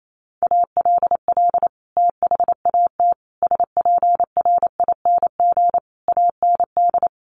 Morse code, heard as a keyed tone, about 28 wpm, 700 Hz. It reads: ALL THAT SPRING AND
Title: Sentences from Top 1200 Words 28wpm